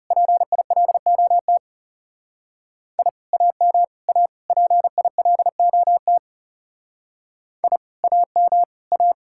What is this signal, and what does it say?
Morse code, heard as a keyed tone, about 37 wpm, 700 Hz. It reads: PILOT I AM A PILOT I AM A
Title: Speed-Racing Common QSO Phrases - 30wpm